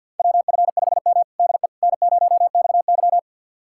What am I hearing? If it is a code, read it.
Morse code, heard as a keyed tone, about 53 wpm, 700 Hz. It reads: WF5M DE N0XY
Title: USA Callsign DE Callsign - 4 Characters - 50wpm